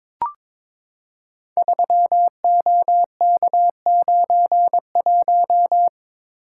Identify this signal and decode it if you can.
Morse code, heard as a keyed tone, about 23 wpm, 690 Hz. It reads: 3OK91
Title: Code Groups - 5 Letter:Numbers - 22wpm